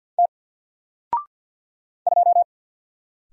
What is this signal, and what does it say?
Morse code, heard as a keyed tone, about 48 wpm, 700 Hz. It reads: T 2
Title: Single Letter-Number Alt 50wpm